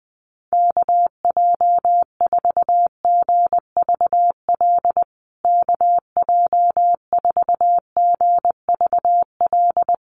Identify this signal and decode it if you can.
Morse code, heard as a keyed tone, about 19 wpm, 700 Hz. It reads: KJ4GVL KJ4GVL
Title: USA Call Signs Repeated 3x 20wpm